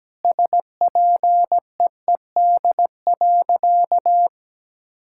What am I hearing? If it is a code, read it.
Morse code, heard as a keyed tone, about 17 wpm, 700 Hz. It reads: SPEED.
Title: Sentences from Top 1000 Words - Review Part 5 - 17wpm